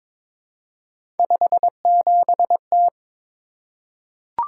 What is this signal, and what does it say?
Morse code, heard as a keyed tone, about 23 wpm, 700 Hz. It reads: 57T
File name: practice / 22wpm / Three Letter-Number Combinations - 22wpm